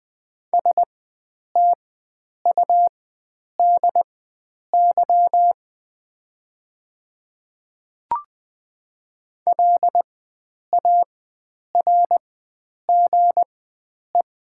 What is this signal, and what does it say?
Morse code, heard as a keyed tone, about 20 wpm, spaced as for 9 wpm, 700 Hz. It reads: STUDY LARGE
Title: 4x Character Spacing - Top 300 Words - 20wpm